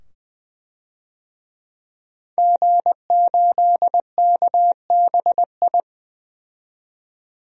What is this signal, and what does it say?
Morse code, heard as a keyed tone, about 21 wpm, 700 Hz. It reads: G8KBI
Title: DXCC-100 Call Signs - 5 Characters - 20wpm